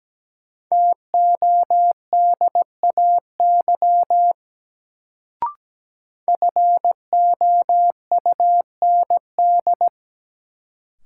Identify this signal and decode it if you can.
Morse code, heard as a keyed tone, about 17 wpm, 700 Hz. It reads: TODAY FOUND
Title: Sets of 5 Words Encore - New Word Drill - 17wpm